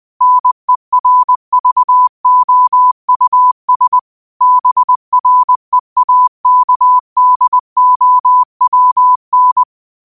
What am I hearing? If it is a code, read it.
Morse code, heard as a keyed tone, about 20 wpm, 1000 Hz. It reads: NERVOUS BREAKDOWN
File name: qotdfiles / 20wpm